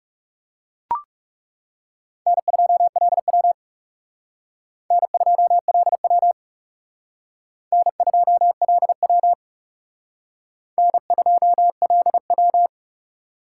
Morse code, heard as a keyed tone, about 38 wpm, 700 Hz. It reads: N2LW N2LW N2LW N2LW